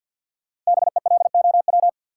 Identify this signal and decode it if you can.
Morse code, heard as a keyed tone, about 47 wpm, 700 Hz. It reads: BELOW
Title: Sentences from Top 1000 Words - Review Part 5 - New Word Drill - 50wpm